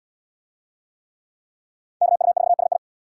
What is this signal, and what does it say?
Morse code, heard as a keyed tone, about 68 wpm, 700 Hz. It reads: KU4SI